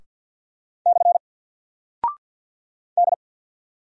Morse code, heard as a keyed tone, about 48 wpm, 700 Hz. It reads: / D
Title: Single Character (Letter:Number:Punctuation:Prosign) Alt 50wpm